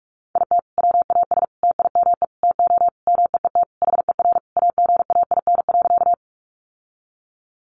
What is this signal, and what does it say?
Morse code, heard as a keyed tone, about 44 wpm, 700 Hz. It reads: IT WAS TIME TO MEET HER AGAIN.
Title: Sentences from Top 1100 Words 45wpm